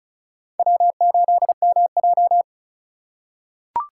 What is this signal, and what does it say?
Morse code, heard as a keyed tone, about 34 wpm, 700 Hz. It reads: W8MJ
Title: USA Call Signs - 4 Characters - 3x Character Spacing - 35wpm